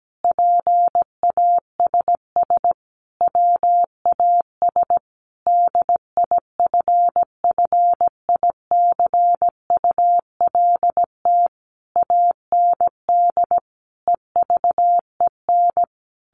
Morse code, heard as a keyed tone, about 17 wpm, 700 Hz. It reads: PASS WAS DIFFICULT AND EVEN
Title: Sentences from Top 1200 Words 17wpm